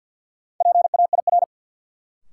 Morse code, heard as a keyed tone, about 47 wpm, 700 Hz. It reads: PAIR
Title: Sentences from Top 1200 Words - New Word Drill - 50wpm